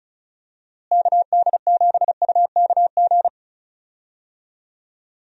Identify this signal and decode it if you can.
Morse code, heard as a keyed tone, about 34 wpm, 700 Hz. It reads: KD7UKG
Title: USA Call Signs - 6 Characters - Repeated 2x - 35wpm